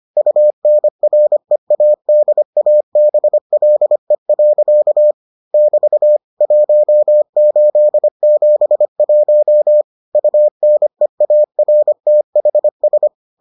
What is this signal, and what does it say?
Morse code, heard as a keyed tone, about 25 wpm, 600 Hz. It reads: UNREADABLE. <BT> 1871 UNEARTHS